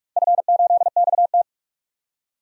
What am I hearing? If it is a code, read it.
Morse code, heard as a keyed tone, about 42 wpm, 700 Hz. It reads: F8XT
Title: DXCC-100 Call Signs - 4 Characters - 45wpm